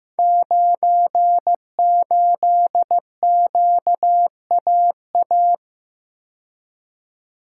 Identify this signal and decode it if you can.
Morse code, heard as a keyed tone, about 15 wpm, 700 Hz. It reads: 98QAA
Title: Code Groups - 5 Letter:Numbers - 15wpm